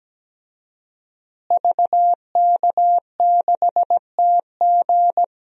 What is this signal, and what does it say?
Morse code, heard as a keyed tone, about 17 wpm, 700 Hz. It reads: VK6TG
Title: DXCC-100 Call Signs - 5 Characters - Repeated 3x - 17wpm